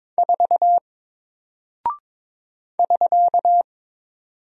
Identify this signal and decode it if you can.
Morse code, heard as a keyed tone, about 22 wpm, 700 Hz. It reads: 4 <SK>
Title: Single Character (Letter:Number:Punctuation:Prosign) 22wpm